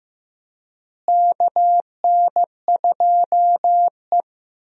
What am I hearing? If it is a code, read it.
Morse code, heard as a keyed tone, about 15 wpm, 700 Hz. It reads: KN2E
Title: USA Call Signs - 4 Characters - Repeated 3x - 15wpm